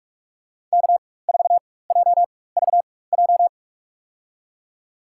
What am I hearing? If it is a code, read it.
Morse code, heard as a keyed tone, about 46 wpm, spaced as for 21 wpm, 700 Hz. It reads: K4JVJ
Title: USA Call Signs - 5 Characters - 4x Character Spacing - 45wpm